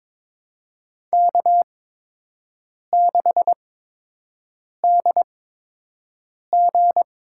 Morse code, heard as a keyed tone, about 22 wpm, spaced as for 6 wpm, 700 Hz. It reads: K6DG
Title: USA Call Signs - 8x Character Spacing - 22wpm